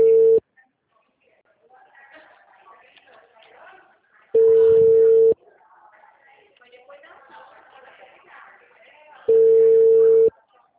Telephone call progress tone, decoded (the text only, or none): ringback tone